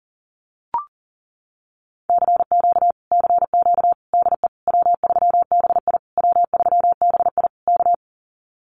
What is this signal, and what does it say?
Morse code, heard as a keyed tone, about 40 wpm, 700 Hz. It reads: CQ CQ DE W3BI W3BI K
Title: Common QSO Phrases Encore 40wpm